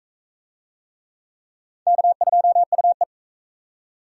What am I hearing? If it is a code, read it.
Morse code, heard as a keyed tone, about 41 wpm, 700 Hz. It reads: K2UE